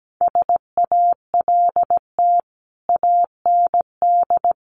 Morse code, heard as a keyed tone, about 17 wpm, 700 Hz. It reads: SALT AND
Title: Sets of 5 Words 17wpm